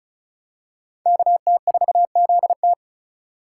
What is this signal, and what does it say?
Morse code, heard as a keyed tone, about 36 wpm, 700 Hz. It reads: KT4ZT
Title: USA Call Signs - 5 Characters - 4x Character Spacing - 35wpm